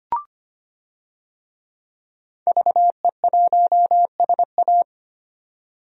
Morse code, heard as a keyed tone, about 25 wpm, 700 Hz. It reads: VE1SA
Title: DXCC-100 Call Signs - 5 Characters - 25wpm